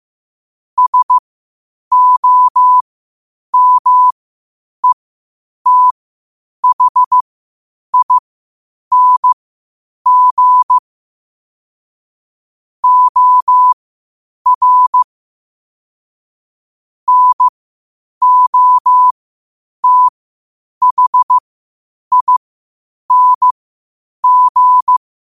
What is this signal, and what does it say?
Morse code, heard as a keyed tone, about 15 wpm, spaced as for 8 wpm, 1000 Hz. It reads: SOMETHING OR NOTHING
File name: qotdfiles / 8wpm